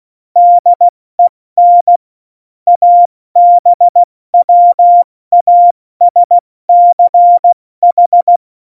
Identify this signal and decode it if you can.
Morse code, heard as a keyed tone, about 16 wpm, 700 Hz. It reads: DEN ABWASCH
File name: Morsekurs / Witze / Witz 20. August 2019, 70 F80